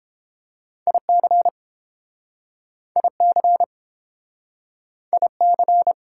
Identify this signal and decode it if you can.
Morse code, heard as a keyed tone, about 32 wpm, 700 Hz. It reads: IC IC IC